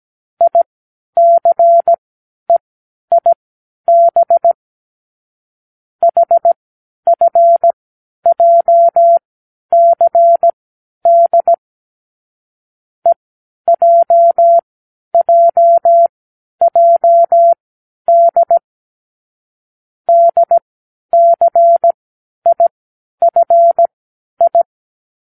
Morse code, heard as a keyed tone, about 17 wpm, spaced as for 10 wpm, 680 Hz. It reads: ICEIB HFJCD EJJJD DCIFI
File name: k7qo / Lesson15(A-J)